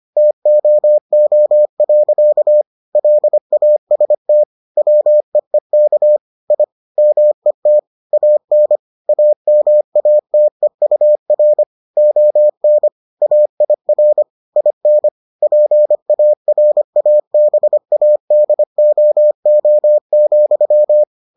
Morse code, heard as a keyed tone, about 25 wpm, 600 Hz. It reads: TOO. LAST WEEK I MET AN AMATEUR ON AIR IN PARABADOO,